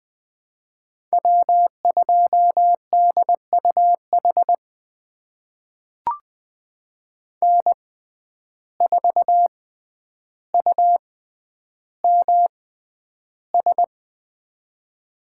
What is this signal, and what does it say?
Morse code, heard as a keyed tone, about 20 wpm, spaced as for 10 wpm, 700 Hz. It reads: W2DUH N4UMS
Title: USA Call Signs - 5 Characters - 6x Character Spacing - 20wpm